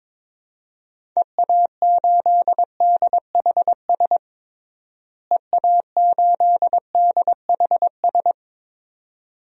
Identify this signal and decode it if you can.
Morse code, heard as a keyed tone, about 22 wpm, 700 Hz. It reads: EA8DHS EA8DHS
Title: DXCC-100 Call Signs - 6 Characters - Repeated 2x - 22wpm